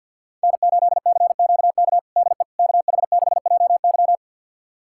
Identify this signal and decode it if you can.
Morse code, heard as a keyed tone, about 50 wpm, 690 Hz. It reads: N8CQK DE KH6JY
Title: USA Callsign DE Callsign - 5 Characters - 50wpm